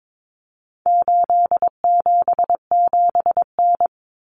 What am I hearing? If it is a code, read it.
Morse code, heard as a keyed tone, about 22 wpm, 700 Hz. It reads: 877N
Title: Common Cut Numbers - 4-Digit 22wpm